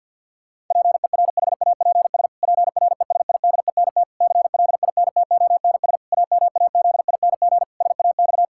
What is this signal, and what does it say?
Morse code, heard as a keyed tone, about 50 wpm, 700 Hz. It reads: PERHAPS PRESIDENT CLINTONS AMAZING SAX
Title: Pangram Challenge v2 - 50wpm